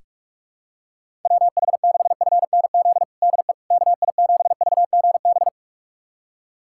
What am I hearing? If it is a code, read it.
Morse code, heard as a keyed tone, about 46 wpm, 700 Hz. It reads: WH6FNZ DE KI7VGB
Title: USA Callsign DE Callsign - 6 Characters - 45wpm